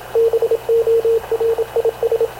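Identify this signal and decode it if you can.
Morse code, heard as a keyed tone, about 26 wpm, 460 Hz, with radio noise. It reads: BORIS
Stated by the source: Havana Radio CLA press (PX) broadcast, recorded by Radio Officer Paul Dumesnil aboard mv CNAV QUEST/CZDO.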